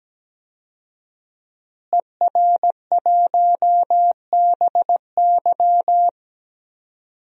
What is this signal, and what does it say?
Morse code, heard as a keyed tone, about 17 wpm, 700 Hz. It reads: ER1BY
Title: DXCC-100 Call Signs - 5 Characters - Repeated 2x - 17wpm